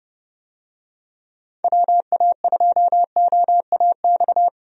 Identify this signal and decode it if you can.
Morse code, heard as a keyed tone, about 30 wpm, 700 Hz. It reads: WA2OAX
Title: USA Call Signs - 6 Characters - 30wpm